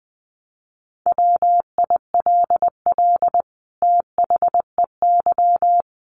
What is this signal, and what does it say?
Morse code, heard as a keyed tone, about 20 wpm, 700 Hz. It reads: WILL THEY
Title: Sentences from Top 1000 Words 20wpm